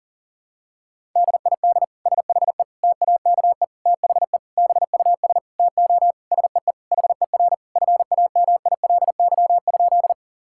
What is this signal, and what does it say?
Morse code, heard as a keyed tone, about 40 wpm, 700 Hz. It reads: DID SHE TAKE THE BUS TO SEE HER FAMILY?
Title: Sentences from Top 1000 Words - Review Part 6 - 40wpm